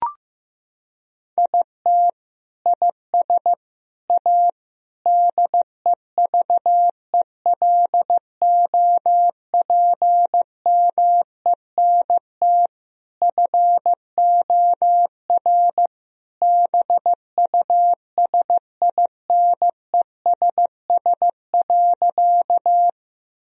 Morse code, heard as a keyed tone, about 15 wpm, 700 Hz. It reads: IT IS A DEVELOPMENT FOR BUSINESS.
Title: Sentences from Top 1000 Words - Review Part 4 - 15wpm